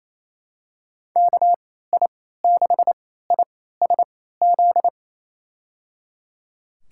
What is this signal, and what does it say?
Morse code, heard as a keyed tone, about 27 wpm, spaced as for 16 wpm, 700 Hz. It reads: KI6ISZ
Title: USA Call Signs - 3x Character Spacing - 28wpm